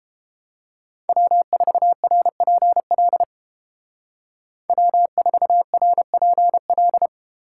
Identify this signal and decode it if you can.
Morse code, heard as a keyed tone, about 29 wpm, 700 Hz. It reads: W4RPL W4RPL